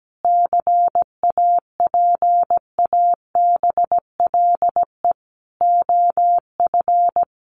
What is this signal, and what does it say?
Morse code, heard as a keyed tone, about 17 wpm, 700 Hz. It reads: CAPABLE OF
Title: Sentences from Top 1200 Words 17wpm